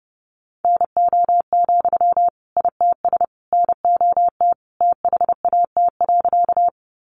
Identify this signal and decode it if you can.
Morse code, heard as a keyed tone, about 28 wpm, 700 Hz. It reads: NO, ITS NOT THAT.
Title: Sentences from Top 600 Words 30wpm